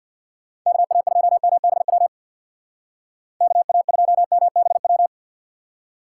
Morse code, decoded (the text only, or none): KA2MBW KA2MBW